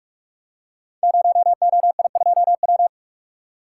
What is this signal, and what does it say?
Morse code, heard as a keyed tone, about 44 wpm, 700 Hz. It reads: 0OI2W
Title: Code Groups - 5 Letter:Numbers - 45wpm